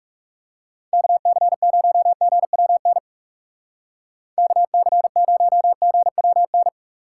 Morse code, heard as a keyed tone, about 37 wpm, 700 Hz. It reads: KC0GWN KC0GWN
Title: Speed Racing - USA 2x3 Call Signs 30wpm